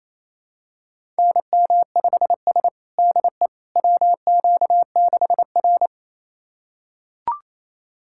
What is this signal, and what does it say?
Morse code, heard as a keyed tone, about 28 wpm, 700 Hz. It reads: NM5S DE WQ6R
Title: USA Callsign DE Callsign - 4 Characters - 28wpm